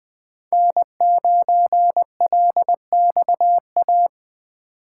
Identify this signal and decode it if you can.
Morse code, heard as a keyed tone, about 20 wpm, 700 Hz. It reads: N9LXA
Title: USA Call Signs - 5 Characters - 4x Character Spacing - 20wpm